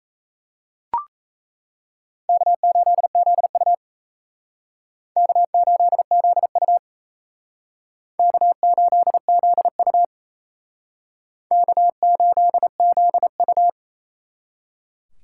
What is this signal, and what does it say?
Morse code, heard as a keyed tone, about 36 wpm, 700 Hz. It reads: K8ZU K8ZU K8ZU K8ZU